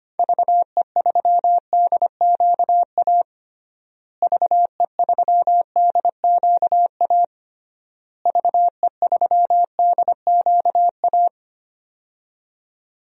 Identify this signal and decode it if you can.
Morse code, heard as a keyed tone, about 25 wpm, 700 Hz. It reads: VE3DQA VE3DQA VE3DQA
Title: DXCC-100 Call Signs - 6 Characters - Repeated 3x - 25wpm